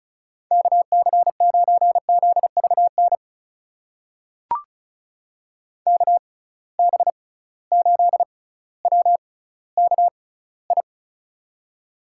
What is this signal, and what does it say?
Morse code, heard as a keyed tone, about 36 wpm, spaced as for 18 wpm, 700 Hz. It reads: KC9ZVN KB8WKI
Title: USA Call Signs - 6 Characters - 6x Character Spacing - 35wpm